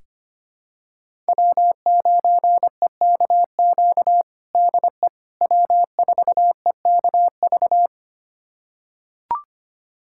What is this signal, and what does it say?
Morse code, heard as a keyed tone, about 25 wpm, 700 Hz. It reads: W9EKQ DE W4EKV
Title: USA Callsign DE Callsign - 5 Characters - 25wpm